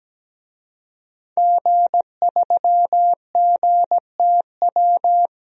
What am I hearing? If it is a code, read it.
Morse code, heard as a keyed tone, about 17 wpm, 700 Hz. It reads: G3GTW
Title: DXCC-100 Call Signs - 5 Characters - 17wpm